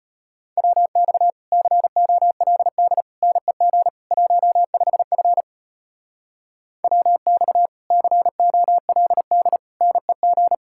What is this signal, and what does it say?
Morse code, heard as a keyed tone, about 33 wpm, 700 Hz. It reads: WX COLD NEG 15F WX COLD NEG